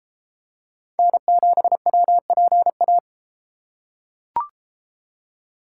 Morse code, heard as a keyed tone, about 31 wpm, 700 Hz. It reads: N7WPA